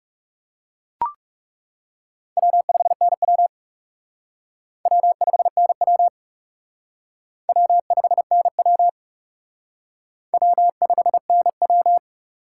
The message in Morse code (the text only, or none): W5NW W5NW W5NW W5NW